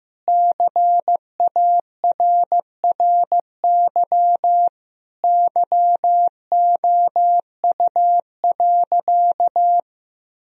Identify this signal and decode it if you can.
Morse code, heard as a keyed tone, about 15 wpm, 700 Hz. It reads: CARRY YOU.
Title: Sentences from Top 1000 Words - Review Part 1 - 15wpm